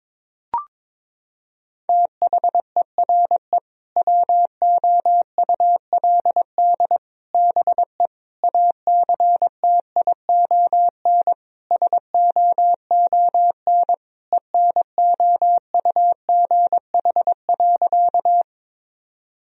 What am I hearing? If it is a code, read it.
Morse code, heard as a keyed tone, about 22 wpm, 700 Hz. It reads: THERE WOULD BE ACTION SOON ENOUGH.
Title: Sentences from Top 1000 Words - Review Part 4 - 22wpm